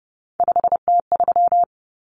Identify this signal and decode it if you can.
Morse code, heard as a keyed tone, about 30 wpm, 700 Hz. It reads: 5T3
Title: Common Cut Numbers - 3-Digit 30wpm